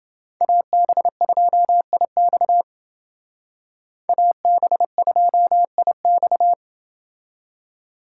Morse code, heard as a keyed tone, about 30 wpm, 700 Hz. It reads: AB2IX AB2IX